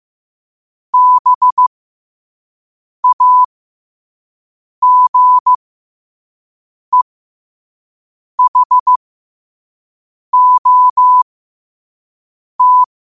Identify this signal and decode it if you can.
Morse code, heard as a keyed tone, about 15 wpm, spaced as for 5 wpm, 1000 Hz. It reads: BAGEHOT
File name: qotdfiles / 5wpm